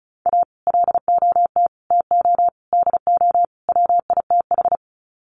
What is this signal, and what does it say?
Morse code, heard as a keyed tone, about 34 wpm, 700 Hz. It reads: A LOT TO DO WITH
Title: Sets of 5 Words 35wpm